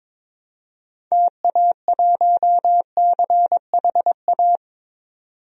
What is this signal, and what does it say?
Morse code, heard as a keyed tone, about 21 wpm, 700 Hz. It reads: TA1CHA
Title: DXCC-100 Call Signs - 6 Characters - 22wpm